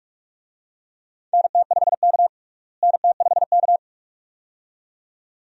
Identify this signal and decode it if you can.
Morse code, heard as a keyed tone, about 46 wpm, 700 Hz. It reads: NT5K NT5K
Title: USA Call Signs - 4 Characters - Repeated 2x - 45wpm